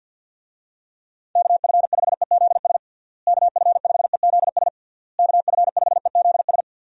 Morse code, heard as a keyed tone, about 49 wpm, 700 Hz. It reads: KF5EZS KF5EZS KF5EZS
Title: USA Call Signs - 6 Characters - Repeated 3x - 50wpm